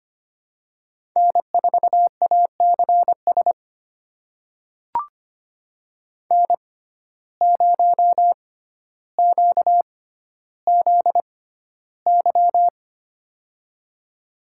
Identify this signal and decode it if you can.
Morse code, heard as a keyed tone, about 25 wpm, spaced as for 13 wpm, 700 Hz. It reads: N4ACS N0QZY
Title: USA Call Signs - 5 Characters - 6x Character Spacing - 25wpm